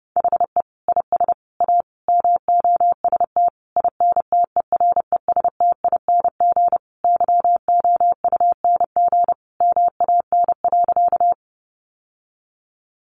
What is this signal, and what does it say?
Morse code, heard as a keyed tone, about 30 wpm, 700 Hz. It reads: HE IS A MOST INTERESTING YOUNG MAN.
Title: Sentences from Top 1000 Words 30wpm